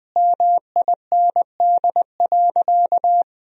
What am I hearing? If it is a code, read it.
Morse code, heard as a keyed tone, about 20 wpm, 700 Hz. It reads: MIND.
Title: Sentences from Top 1000 Words - Review Part 3 - 20wpm